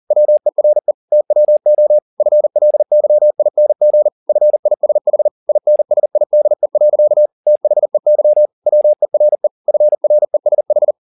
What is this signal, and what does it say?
Morse code, heard as a keyed tone, about 40 wpm, 600 Hz. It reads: WERE TWO FLYING FISH INSIDE. THEY WERE FRESH
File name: the old man and the sea 2 40wpm